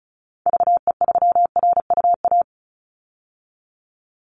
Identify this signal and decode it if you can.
Morse code, heard as a keyed tone, about 34 wpm, 700 Hz. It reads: VE3RUA
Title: Call Signs Repeated 2x - 35wpm